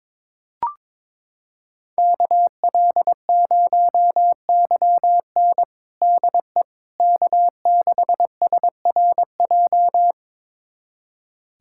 Morse code, heard as a keyed tone, about 22 wpm, 700 Hz. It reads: KL0YN DE K6SRJ
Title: USA Callsign DE Callsign - 5 Characters - 22wpm